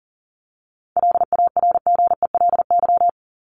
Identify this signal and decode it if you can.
Morse code, heard as a keyed tone, about 39 wpm, 700 Hz. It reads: LARGELY
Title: LY Suffix - Top 100 Words with Base Words 40wpm